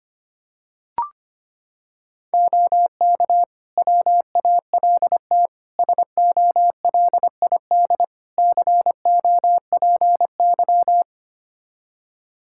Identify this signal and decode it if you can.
Morse code, heard as a keyed tone, about 25 wpm, 700 Hz. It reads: OK WALT SOLID COPY
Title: Practice QSO - Exchange 3 - More Info - 25wpm